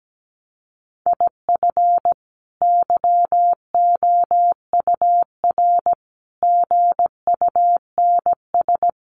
Morse code, heard as a keyed tone, about 17 wpm, 700 Hz. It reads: IF YOUR GUNS
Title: A Sound Of Thunder 17wpm